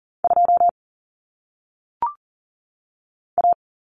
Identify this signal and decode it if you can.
Morse code, heard as a keyed tone, about 40 wpm, 700 Hz. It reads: 2 A
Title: Common Cut Numbers - 1-Digit 40wpm